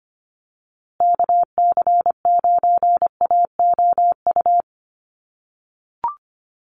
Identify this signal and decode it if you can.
Morse code, heard as a keyed tone, about 25 wpm, 700 Hz. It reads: KC9AOU
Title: USA Call Signs Repeated 3x 25wpm